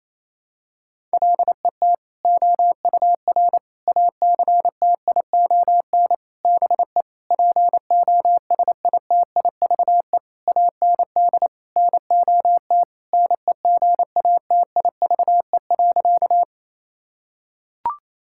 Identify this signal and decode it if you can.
Morse code, heard as a keyed tone, about 28 wpm, 700 Hz. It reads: LET OUR ACTION BE POSITIVE AND NOT NEGATIVE.
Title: Sentences from Top 1000 Words - Review Part 6 - 28wpm